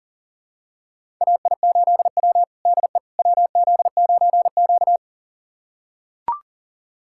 Morse code, heard as a keyed tone, about 40 wpm, 700 Hz. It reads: AI8W DE WZ9Q
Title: USA Callsign DE Callsign - 4 Characters - 40wpm